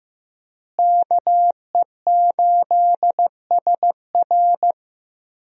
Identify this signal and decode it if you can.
Morse code, heard as a keyed tone, about 14 wpm, 700 Hz. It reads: KE8SR
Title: USA Call Signs - 5 Characters - Repeated 3x - 15wpm